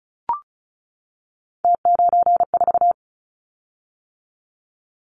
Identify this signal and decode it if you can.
Morse code, heard as a keyed tone, about 36 wpm, 700 Hz. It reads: T94
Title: Callsign Prefixes for Top 50 DXCC - 35wpm